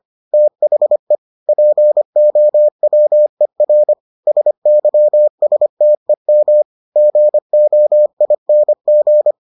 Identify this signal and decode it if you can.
Morse code, heard as a keyed tone, about 25 wpm, 600 Hz. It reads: THE POWER SYSTEM GOING